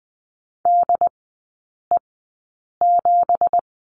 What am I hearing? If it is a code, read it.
Morse code, heard as a keyed tone, about 20 wpm, spaced as for 8 wpm, 700 Hz. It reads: DE7
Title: 178 Lesson 29e - Slash - Three Characters 20wpm